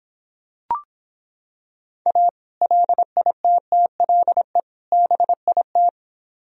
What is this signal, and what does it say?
Morse code, heard as a keyed tone, about 26 wpm, 700 Hz. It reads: A LITTLE BIT